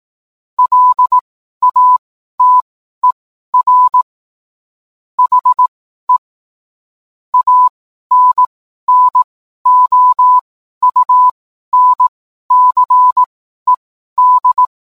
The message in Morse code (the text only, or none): LATER HE ANNOUNCED